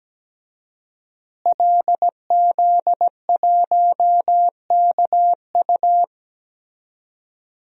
Morse code, heard as a keyed tone, about 17 wpm, 700 Hz. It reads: LZ1KU
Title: DXCC-100 Call Signs - 5 Characters - Repeated 3x - 17wpm